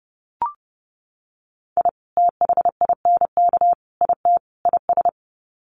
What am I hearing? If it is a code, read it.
Morse code, heard as a keyed tone, about 30 wpm, 700 Hz. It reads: I THINK IT IS
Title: Sets of 4 Words 30wpm